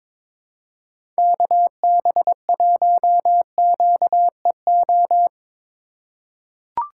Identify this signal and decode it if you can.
Morse code, heard as a keyed tone, about 22 wpm, 700 Hz. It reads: KB1QEO
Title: USA Call Signs - 6 Characters - 2x Character Spacing - 22wpm